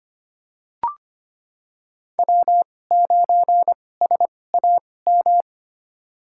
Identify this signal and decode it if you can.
Morse code, heard as a keyed tone, about 25 wpm, spaced as for 18 wpm, 700 Hz. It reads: W9SAM
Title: USA Call Signs - 5 Characters - 2x Character Spacing - 25wpm